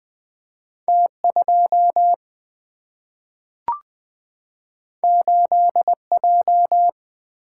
Morse code, heard as a keyed tone, about 20 wpm, 700 Hz. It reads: T 2 8 J
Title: Single-Letter Single-Number 20wpm